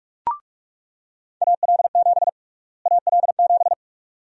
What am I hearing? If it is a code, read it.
Morse code, heard as a keyed tone, about 41 wpm, 700 Hz. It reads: AL7 AL7
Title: USA Call Sign Prefixes - Repeated 2x - 45wpm